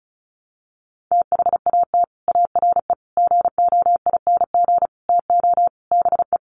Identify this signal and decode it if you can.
Morse code, heard as a keyed tone, about 34 wpm, 700 Hz. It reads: THAT ARE GOING TO BE
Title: Sets of 5 Words Encore2 35wpm